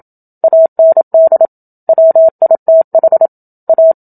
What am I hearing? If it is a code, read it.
Morse code, heard as a keyed tone, about 27 wpm, 650 Hz. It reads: AND WITH A
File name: Book Two / 01-Chapter 1 - 27 WPM